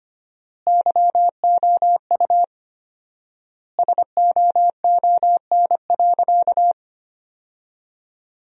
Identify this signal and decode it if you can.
Morse code, heard as a keyed tone, about 25 wpm, 700 Hz. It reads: YOU SOON.
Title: 4x Word-Spacing - Sentences from Top 400 Words - 25wpm